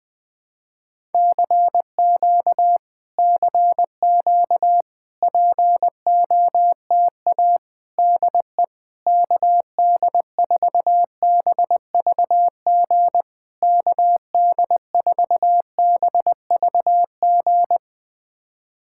Morse code, heard as a keyed tone, about 20 wpm, 700 Hz. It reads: CQ CQ POTA DE KD4BVG KD4BVG
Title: POTA QSO Practice - 20wpm